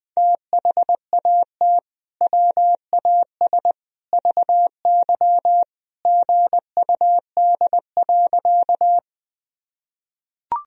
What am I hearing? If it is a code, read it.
Morse code, heard as a keyed tone, about 21 wpm, 700 Hz. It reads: THAT WAS VY GUD.
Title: Sentences from Top 200 Words and Abbreviations 20wpm